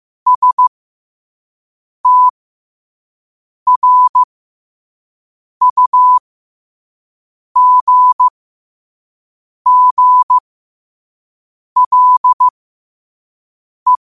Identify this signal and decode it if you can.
Morse code, heard as a keyed tone, about 15 wpm, spaced as for 5 wpm, 1000 Hz. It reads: STRUGGLE
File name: qotdfiles / 5wpm